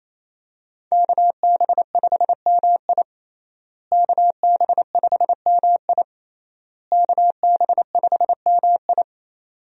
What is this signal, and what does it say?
Morse code, heard as a keyed tone, about 27 wpm, 700 Hz. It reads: KB5MI KB5MI KB5MI
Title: USA Call Signs - 5 Characters - Repeated 3x - 28wpm